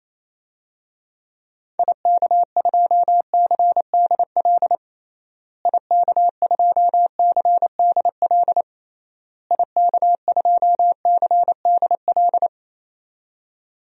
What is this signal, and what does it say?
Morse code, heard as a keyed tone, about 28 wpm, 700 Hz. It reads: IK2CDL IK2CDL IK2CDL
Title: DXCC-100 Call Signs - 6 Characters - Repeated 3x - 28wpm